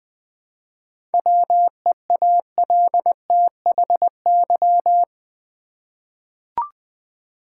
Morse code, heard as a keyed tone, about 20 wpm, 700 Hz. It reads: WEALTHY
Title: 7 Letter Words - First 3 Letters Given - 20wpm